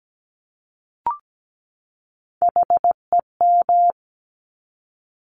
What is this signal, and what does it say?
Morse code, heard as a keyed tone, about 17 wpm, 690 Hz. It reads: HEM
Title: Top Three Letter Combinations in Words 17wpm